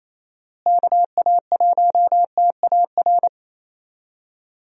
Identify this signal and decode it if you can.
Morse code, heard as a keyed tone, about 27 wpm, 700 Hz. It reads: KA1TAR
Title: USA Call Signs - 6 Characters - Repeated 2x - 28wpm